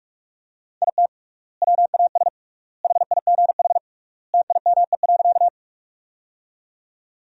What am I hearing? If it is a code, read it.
Morse code, heard as a keyed tone, about 44 wpm, 700 Hz. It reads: IT WAS HIGH TIME.
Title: 3x Word-Spacing - Sentences from Top 200 Words - 45wpm